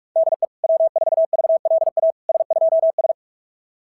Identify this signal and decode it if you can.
Morse code, heard as a keyed tone, about 46 wpm, 690 Hz. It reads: DE W4VLA S2S
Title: SOTA QSO Practice 45wpm